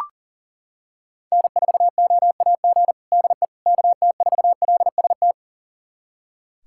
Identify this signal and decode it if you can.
Morse code, heard as a keyed tone, about 40 wpm, 700 Hz. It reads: N4OAG DE KT4LST
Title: USA Callsign DE Callsign 40wpm